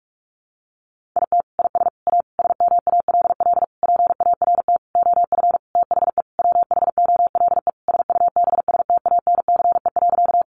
Morse code, heard as a keyed tone, about 44 wpm, 700 Hz. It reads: IT IS A SMALL PART OF THE WHOLE SUBSTANCE.
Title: Sentences from Top 1500 Words 45wpm